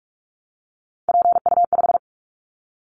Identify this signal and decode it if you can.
Morse code, heard as a keyed tone, about 46 wpm, 700 Hz. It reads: PU5
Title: Callsign Prefixes for Top 100 DXCC - 45wpm